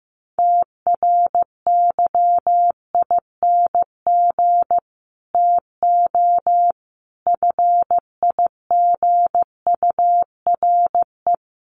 Morse code, heard as a keyed tone, about 14 wpm, 700 Hz. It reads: TRYING TO FIGURE
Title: Sets of 5 Words Encore 15wpm